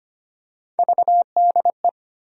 Morse code, heard as a keyed tone, about 25 wpm, 700 Hz. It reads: VDE
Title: USA Call Sign Postfixes - 3 Letters 25wpm